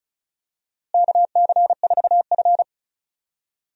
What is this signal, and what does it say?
Morse code, heard as a keyed tone, about 36 wpm, 700 Hz. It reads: KC4F
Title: USA Call Signs - 4 Characters - Repeated 2x - 35wpm